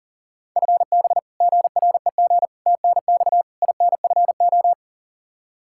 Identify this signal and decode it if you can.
Morse code, heard as a keyed tone, about 39 wpm, 700 Hz. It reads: FB GREG TNX INFO
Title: Practice QSO - Exchange 3 - More Info - 40wpm